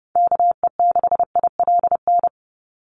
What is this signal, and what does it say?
Morse code, heard as a keyed tone, about 30 wpm, 700 Hz. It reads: KE6ILN
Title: USA Call Signs 30wpm